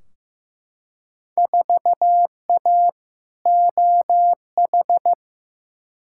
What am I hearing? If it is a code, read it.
Morse code, heard as a keyed tone, about 15 wpm, 700 Hz. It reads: 4A OH
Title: Field Day Exchanges - 15wpm